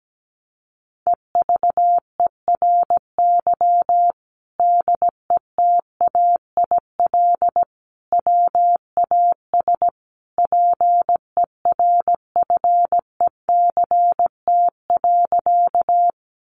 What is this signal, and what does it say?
Morse code, heard as a keyed tone, about 17 wpm, 700 Hz. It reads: EVERY DETAIL WAS PERFECT.
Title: Sentences from Top 1500 Words 17wpm